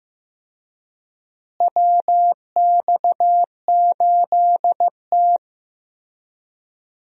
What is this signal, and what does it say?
Morse code, heard as a keyed tone, about 15 wpm, 700 Hz. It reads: WX8T
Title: USA Call Signs - 4 Characters - Repeated 3x - 15wpm